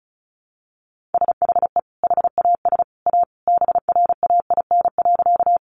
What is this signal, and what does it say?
Morse code, heard as a keyed tone, about 34 wpm, 700 Hz. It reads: SHE HAS A BRAIN.
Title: Sentences from Top 1100 Words 35wpm